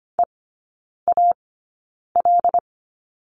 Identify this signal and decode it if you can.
Morse code, heard as a keyed tone, about 25 wpm, spaced as for 9 wpm, 690 Hz. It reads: EAL
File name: Lesson 14e - Number 2 - Three Characters 25wpm